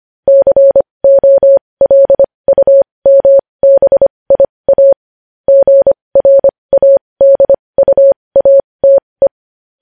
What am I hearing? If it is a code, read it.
Morse code, heard as a keyed tone, about 25 wpm, spaced as for 20 wpm, 550 Hz. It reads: COLUMBIA GRADUATE
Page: news headlines in morse code at 20 wpm